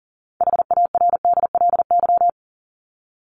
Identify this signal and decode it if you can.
Morse code, heard as a keyed tone, about 40 wpm, 700 Hz. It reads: HARDLY
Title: LY Suffix - Top 100 Words with Base Words 40wpm